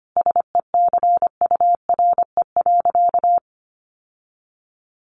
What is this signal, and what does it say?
Morse code, heard as a keyed tone, about 25 wpm, 700 Hz. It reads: SECURE.
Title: Sentences from Top 1200 Words 25wpm